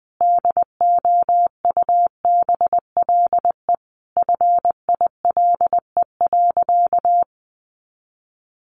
Morse code, heard as a keyed tone, about 20 wpm, 700 Hz. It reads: DOUBLE FILE.
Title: Sentences from Top 1400 Words 20wpm